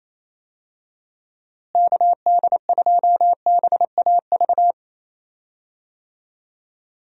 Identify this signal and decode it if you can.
Morse code, heard as a keyed tone, about 28 wpm, 700 Hz. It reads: KD2BAV
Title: USA Call Signs - 6 Characters - Repeated 3x - 28wpm